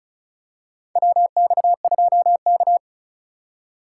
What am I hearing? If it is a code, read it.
Morse code, heard as a keyed tone, about 36 wpm, 700 Hz. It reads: WX2K
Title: USA Call Signs - 4 Characters - 4x Character Spacing - 35wpm